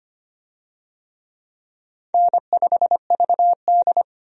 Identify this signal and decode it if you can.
Morse code, heard as a keyed tone, about 25 wpm, 700 Hz. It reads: N5VD
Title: USA Call Signs - 4 Characters - 25wpm